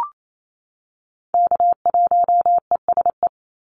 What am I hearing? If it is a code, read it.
Morse code, heard as a keyed tone, about 28 wpm, 700 Hz. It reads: K1ESE
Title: Call Signs - 28wpm